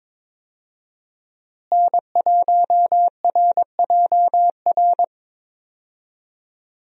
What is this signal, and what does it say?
Morse code, heard as a keyed tone, about 21 wpm, 700 Hz. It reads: N1RJR
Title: USA Call Signs - 5 Characters - Repeated 3x - 22wpm